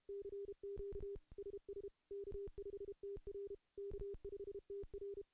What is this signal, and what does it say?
Morse code, heard as a keyed tone, about 33 wpm, 400 Hz. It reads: CQ SS K5TR K5TR
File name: CW-SS-k5tr